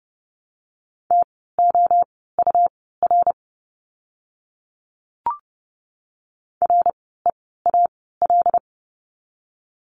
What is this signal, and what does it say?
Morse code, heard as a keyed tone, about 30 wpm, spaced as for 17 wpm, 700 Hz. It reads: TOUR REAL
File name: Lesson 17f - Letter u - Words 30wpm